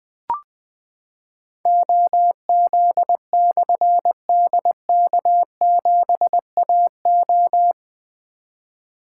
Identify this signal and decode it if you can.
Morse code, heard as a keyed tone, about 20 wpm, 700 Hz. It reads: OZ/DK7AO
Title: Difficult Call Signs 20wpm